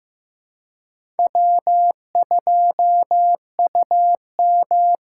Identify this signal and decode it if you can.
Morse code, heard as a keyed tone, about 18 wpm, 700 Hz. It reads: W2UM
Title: Speed Racing - USA 1x2 Call Signs 15wpm